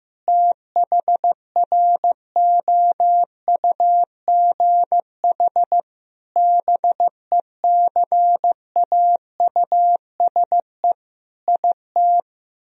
Morse code, heard as a keyed tone, about 15 wpm, 700 Hz. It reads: THROUGH BECAUSE IT
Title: Sentences from Top 1000 Words - Review Part 1 - 15wpm